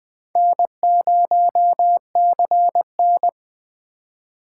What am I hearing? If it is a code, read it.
Morse code, heard as a keyed tone, about 21 wpm, 700 Hz. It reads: N0CN